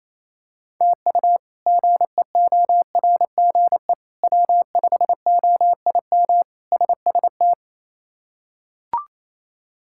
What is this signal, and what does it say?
Morse code, heard as a keyed tone, about 27 wpm, 700 Hz. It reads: TU GEORGE W5OIM SST
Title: K1USN SST QSO Practice - 28wpm